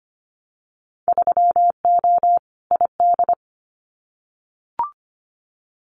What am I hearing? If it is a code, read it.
Morse code, heard as a keyed tone, about 25 wpm, 700 Hz. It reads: 3O ID
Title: Winter Field Day Exchanges 25wpm